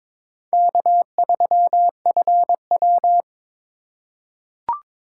Morse code, heard as a keyed tone, about 22 wpm, 700 Hz. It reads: K3FW
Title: USA Call Signs - 4 Characters - 2x Character Spacing - 22wpm